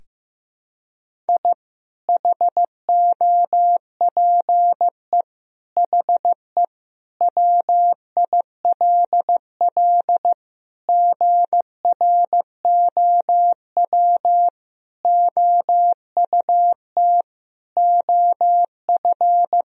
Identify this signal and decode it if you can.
Morse code, heard as a keyed tone, about 15 wpm, 700 Hz. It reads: I HOPE HE WILL GROW OUT OF
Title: Sentences from Top 1000 Words - Review Part 1 - 15wpm